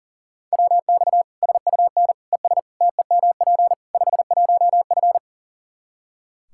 Morse code, heard as a keyed tone, about 40 wpm, 700 Hz. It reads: WX SUN ES TEMP 51F
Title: Practice QSO - Exchange 2 - Info - 40wpm